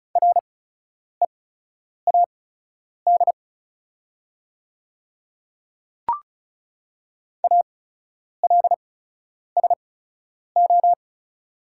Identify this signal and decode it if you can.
Morse code, heard as a keyed tone, about 36 wpm, spaced as for 10 wpm, 700 Hz. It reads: READ ALSO
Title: 8x Character Spacing - Top 200 Words - 35wpm